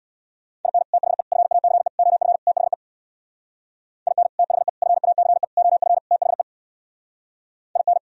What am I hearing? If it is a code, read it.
Morse code, heard as a keyed tone, about 64 wpm, 700 Hz. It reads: IN THE FACE OF THE IN THE FACE OF THE IN